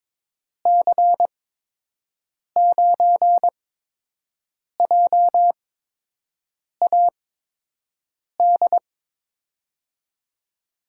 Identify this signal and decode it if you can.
Morse code, heard as a keyed tone, about 22 wpm, spaced as for 6 wpm, 700 Hz. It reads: C9JAD
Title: USA Call Signs - 6 Characters - 8x Character Spacing - 22wpm